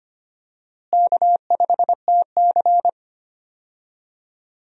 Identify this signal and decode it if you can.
Morse code, heard as a keyed tone, about 25 wpm, 700 Hz. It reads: K5TC
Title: USA Call Signs - 4 Characters - Repeated 2x - 25wpm